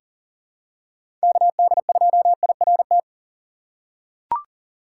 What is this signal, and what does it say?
Morse code, heard as a keyed tone, about 37 wpm, 700 Hz. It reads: KD2IRT
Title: USA Call Signs - 6 Characters - 3x Character Spacing - 40wpm